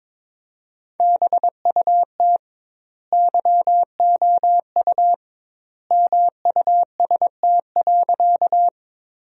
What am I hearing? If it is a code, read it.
Morse code, heard as a keyed tone, about 21 wpm, 700 Hz. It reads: BUT YOU MUST.
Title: 2x Word-Spacing - Sentences from Top 200 Words - 22wpm